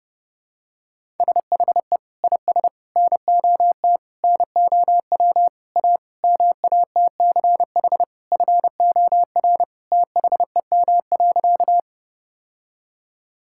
Morse code, heard as a keyed tone, about 30 wpm, 700 Hz. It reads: SHE IS NOT NOW A MATCH FOR THEM.
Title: Sentences from Top 1600 Words 30wpm